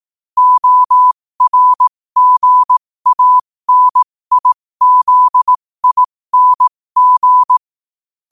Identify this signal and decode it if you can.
Morse code, heard as a keyed tone, about 18 wpm, spaced as for 15 wpm, 1000 Hz. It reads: ORGANIZING
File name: qotdfiles / 15wpm